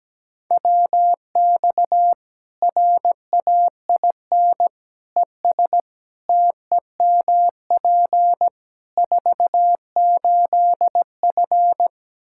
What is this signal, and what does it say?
Morse code, heard as a keyed tone, about 17 wpm, 700 Hz. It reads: WX RAIN ES TEMP 48F
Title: Practice QSO - Exchange 2 - Info - 17wpm